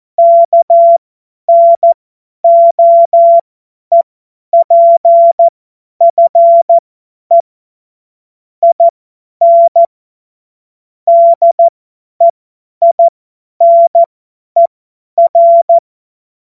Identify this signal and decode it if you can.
Morse code, heard as a keyed tone, about 14 wpm, spaced as for 10 wpm, 680 Hz. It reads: KNOEPFE IN DEINER
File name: Morsekurs / Witze / Witz 03. August 2021, 50 F70